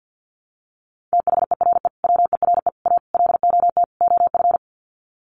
Morse code, heard as a keyed tone, about 50 wpm, 700 Hz. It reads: THERE WERE A LOT OF
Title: Sets of 5 Words 50wpm